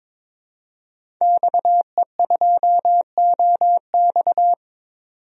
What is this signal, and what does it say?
Morse code, heard as a keyed tone, about 21 wpm, 700 Hz. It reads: XE2OX
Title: DXCC-100 Call Signs - 5 Characters - 22wpm